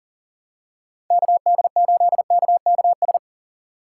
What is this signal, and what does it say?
Morse code, heard as a keyed tone, about 40 wpm, 700 Hz. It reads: KD8KKS
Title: USA Call Signs - 6 Characters - 2x Character Spacing - 40wpm